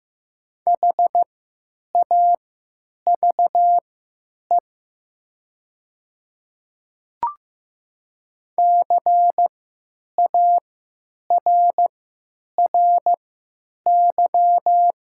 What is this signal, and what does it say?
Morse code, heard as a keyed tone, about 15 wpm, spaced as for 9 wpm, 700 Hz. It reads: HAVE CARRY
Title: 3x Character Spacing - Top 500 Words - 15wpm